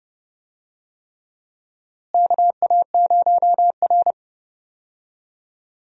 Morse code, heard as a keyed tone, about 30 wpm, 700 Hz. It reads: KA0R
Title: USA Call Signs - 4 Characters - Repeated 3x - 30wpm